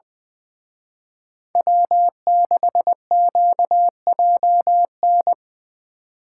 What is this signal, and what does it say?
Morse code, heard as a keyed tone, about 23 wpm, 700 Hz. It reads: W6QJN
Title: Speed Racing - USA 1x3 Call Signs 20wpm